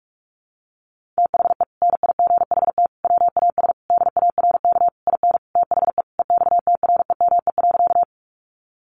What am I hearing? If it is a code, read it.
Morse code, heard as a keyed tone, about 44 wpm, 700 Hz. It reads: THE NIGHT WAS DARK IN THE EXTREME.
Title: Sentences from Top 1300 Words 45wpm